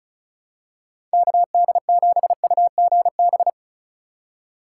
Morse code, heard as a keyed tone, about 36 wpm, 700 Hz. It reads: KD7UGB
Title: USA Call Signs - 6 Characters - 2x Character Spacing - 35wpm